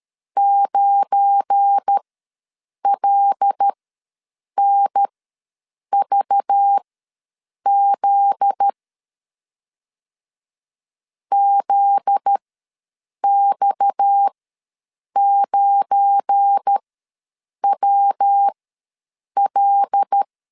Morse code, heard as a keyed tone, about 13 wpm, spaced as for 7 wpm, 800 Hz. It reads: 9LNVZ ZX9WL
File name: LEZIONI / 8 YJ45 / 8 YJ4 ws 12wpm 3249